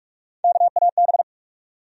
Morse code, heard as a keyed tone, about 47 wpm, 700 Hz. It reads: KAB
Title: USA Call Sign Postfixes - 3 Letters 45wpm